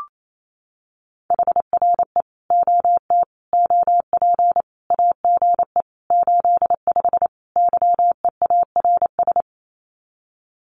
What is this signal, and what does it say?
Morse code, heard as a keyed tone, about 28 wpm, 700 Hz. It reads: HRE OT OP AGE 85 YEARS
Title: Common QSO Phrases 28wpm